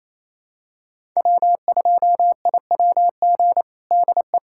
Morse code, heard as a keyed tone, about 28 wpm, 700 Hz. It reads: W2IWG DE
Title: USA Callsign DE Callsign - 5 Characters - 28wpm